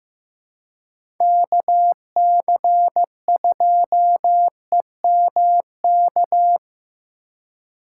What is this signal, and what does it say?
Morse code, heard as a keyed tone, about 15 wpm, 700 Hz. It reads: KC2EMK
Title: USA Call Signs - 3x Character Spacing - 15wpm